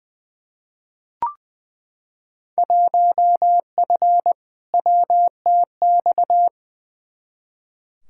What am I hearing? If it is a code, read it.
Morse code, heard as a keyed tone, about 20 wpm, 700 Hz. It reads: 1F WTX
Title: Field Day Exchanges - 20wpm